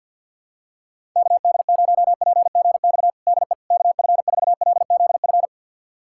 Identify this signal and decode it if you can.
Morse code, heard as a keyed tone, about 50 wpm, 700 Hz. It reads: KD0PGX DE KF4LZF
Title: USA Callsign DE Callsign 50wpm